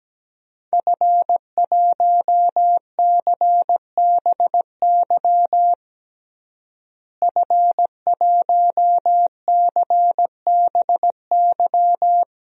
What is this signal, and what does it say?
Morse code, heard as a keyed tone, about 17 wpm, 700 Hz. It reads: F1CBY F1CBY
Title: DXCC-100 Call Signs - 5 Characters - Repeated 3x - 17wpm